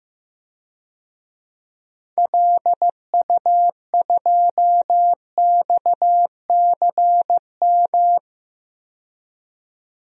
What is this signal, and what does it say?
Morse code, heard as a keyed tone, about 15 wpm, 700 Hz. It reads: LU2XCM
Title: DXCC-100 Call Signs - 6 Characters - 15wpm